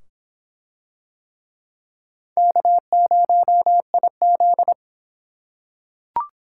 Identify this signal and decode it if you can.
Morse code, heard as a keyed tone, about 24 wpm, 700 Hz. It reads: K0IZ